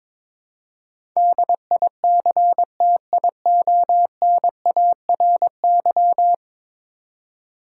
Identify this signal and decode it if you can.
Morse code, heard as a keyed tone, about 22 wpm, 700 Hz. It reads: DICTIONARY
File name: practice / 22wpm / 10 Letter Words - First 3 Letters Given - 22wpm